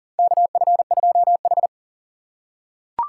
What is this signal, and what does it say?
Morse code, heard as a keyed tone, about 40 wpm, 700 Hz. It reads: KF2H
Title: USA Call Signs - 4 Characters - 6x Character Spacing - 40wpm